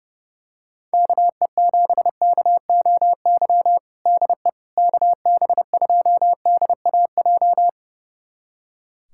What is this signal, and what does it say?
Morse code, heard as a keyed tone, about 30 wpm, 700 Hz. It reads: KE7KOY DE KB2DAJ
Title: USA Callsign DE Callsign - 6 Characters - 30wpm